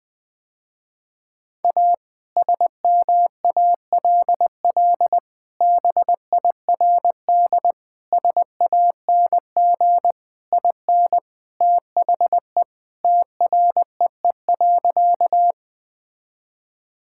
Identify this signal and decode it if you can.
Morse code, heard as a keyed tone, about 20 wpm, 700 Hz. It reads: A SMALL BIRD SANG IN THE TREE.
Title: Sentences with 2-Word Phrases - Spoken Beginning - 20wpm